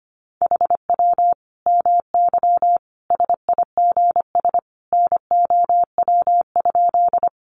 Morse code, heard as a keyed tone, about 25 wpm, 700 Hz. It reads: HW MY SIGS NOW?
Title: Common QSO Phrases 25wpm